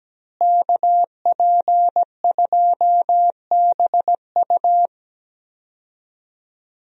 Morse code, heard as a keyed tone, about 17 wpm, 700 Hz. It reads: KP2BU
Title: USA Call Signs - 5 Characters - Repeated 3x - 17wpm